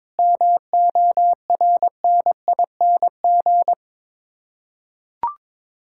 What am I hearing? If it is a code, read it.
Morse code, heard as a keyed tone, about 22 wpm, 700 Hz. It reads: MORNING
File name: practice / 22wpm / Sets of 7 Words - 22wpm